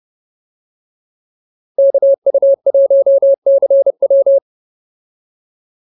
Morse code, heard as a keyed tone, about 30 wpm, 590 Hz. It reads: KU1CW